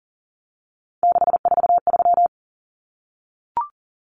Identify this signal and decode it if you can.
Morse code, heard as a keyed tone, about 40 wpm, 700 Hz. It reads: 643
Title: Numbers - 3-Digit 40wpm